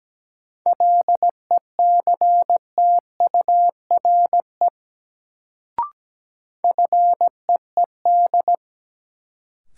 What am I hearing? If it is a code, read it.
Morse code, heard as a keyed tone, about 17 wpm, 700 Hz. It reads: LECTURE FEED
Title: Sentences from Top 1600 Words - New Word Drill - 17wpm